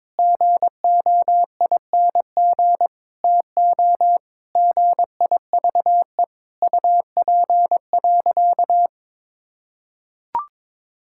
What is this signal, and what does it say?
Morse code, heard as a keyed tone, about 22 wpm, 700 Hz. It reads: GOING TO GIVE UP.
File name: practice / 22wpm / Sentences with 4-Word Phrases - 22wpm